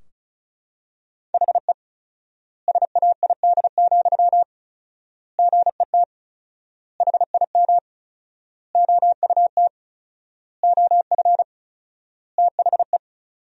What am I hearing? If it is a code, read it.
Morse code, heard as a keyed tone, about 36 wpm, 700 Hz. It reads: HE SAID, GET HIM OUT OF THE
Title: 4x Word-Spacing - Sentences from Top 400 Words - 35wpm